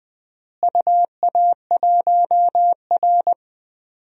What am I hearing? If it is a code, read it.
Morse code, heard as a keyed tone, about 20 wpm, 700 Hz. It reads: UA1R
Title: DXCC-100 Call Signs - 4 Characters - Repeated 3x - 20wpm